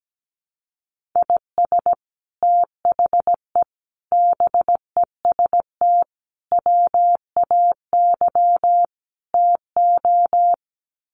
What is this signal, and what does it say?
Morse code, heard as a keyed tone, about 17 wpm, 700 Hz. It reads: IS THE BEST WAY TO
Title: Sets of 5 Words Encore 17wpm